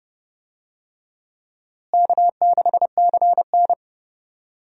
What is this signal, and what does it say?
Morse code, heard as a keyed tone, about 30 wpm, 700 Hz. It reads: K6CN
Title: USA Call Signs - 4 Characters - Repeated 3x - 30wpm